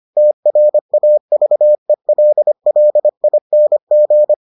Morse code, heard as a keyed tone, about 25 wpm, 600 Hz. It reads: TRAVELLING